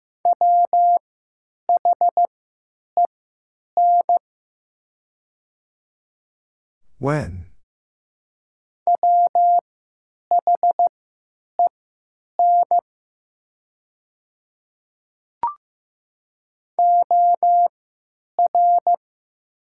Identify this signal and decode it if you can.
Morse code, heard as a keyed tone, about 15 wpm, spaced as for 9 wpm, 700 Hz. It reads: WHEN WHEN OR
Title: 3x Character Spacing - Top 500 Words - 15wpm